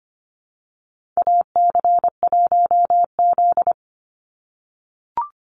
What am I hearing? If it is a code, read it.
Morse code, heard as a keyed tone, about 25 wpm, 700 Hz. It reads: AC1Z
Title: NA SOTA Activator Callsigns 25wpm